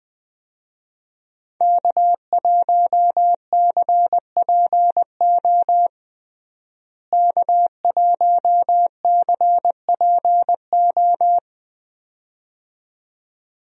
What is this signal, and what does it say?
Morse code, heard as a keyed tone, about 20 wpm, 700 Hz. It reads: K1CPO K1CPO
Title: USA Call Signs - 5 Characters - Repeated 2x - 20wpm